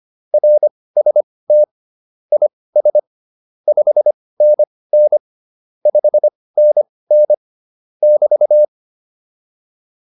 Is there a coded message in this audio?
Yes: Morse code from 0.3 to 8.6 s, heard as a keyed tone, about 25 wpm, spaced as for 18 wpm, 600 Hz. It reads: RST IS 5NN 5NN <BT>